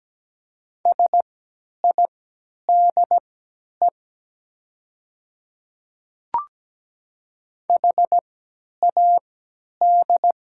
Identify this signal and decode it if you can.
Morse code, heard as a keyed tone, about 17 wpm, spaced as for 10 wpm, 700 Hz. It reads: SIDE HAD
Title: 3x Character Spacing - Top 200 Words - 17wpm